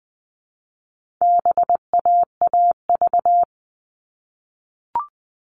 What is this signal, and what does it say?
Morse code, heard as a keyed tone, about 20 wpm, 700 Hz. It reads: BAAV
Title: All Cut Numbers - 4-Digit 20wpm